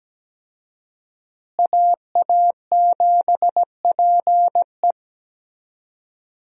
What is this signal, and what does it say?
Morse code, heard as a keyed tone, about 17 wpm, 700 Hz. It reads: AA7PE
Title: USA Call Signs - 5 Characters - Repeated 3x - 17wpm